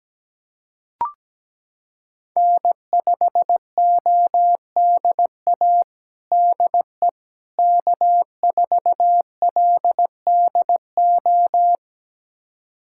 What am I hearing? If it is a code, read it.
Morse code, heard as a keyed tone, about 17 wpm, 700 Hz. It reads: N5ODA DE K4LDO
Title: USA Callsign DE Callsign - 5 Characters - 17wpm